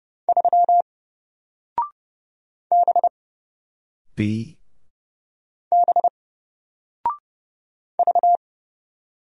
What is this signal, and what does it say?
Morse code, heard as a keyed tone, about 30 wpm, 700 Hz. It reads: 3 B B V
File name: Single-Letter-Number-30wpm